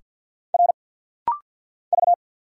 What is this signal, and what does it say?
Morse code, heard as a keyed tone, about 50 wpm, 700 Hz. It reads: R V
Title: Single Letter-Number - Rapid-Fire 50wpm